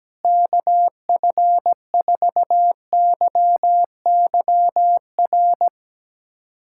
Morse code, heard as a keyed tone, about 17 wpm, 700 Hz. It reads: KF4YYR
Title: USA Callsign DE Callsign 17wpm